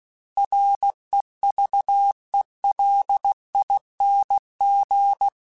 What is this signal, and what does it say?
Morse code, heard as a keyed tone, about 16 wpm, 800 Hz. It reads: REVELING
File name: Alphabetical / Alphabetical mp3s / Rs @15